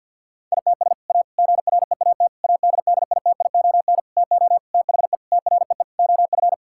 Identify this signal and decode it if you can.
Morse code, heard as a keyed tone, about 50 wpm, 700 Hz. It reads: ITS A GREAT ADDITION TO THE TREE OF
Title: Sentences from Top 1000 Words - Review Part 6 - 50wpm